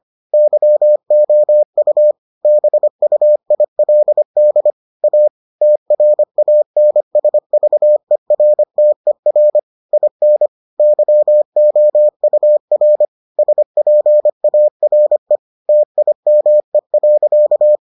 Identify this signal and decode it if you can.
Morse code, heard as a keyed tone, about 25 wpm, 600 Hz. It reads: YOU BUILD A TRANSVERTER IN YOUR SPARE TIME.